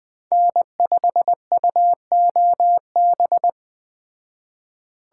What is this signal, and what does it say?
Morse code, heard as a keyed tone, about 20 wpm, 700 Hz. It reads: N5UOB